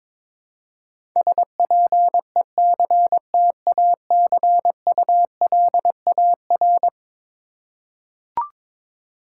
Morse code, heard as a keyed tone, about 22 wpm, 700 Hz. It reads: SPECTACULAR
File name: practice / 22wpm / Eleven Letter Words - 22wpm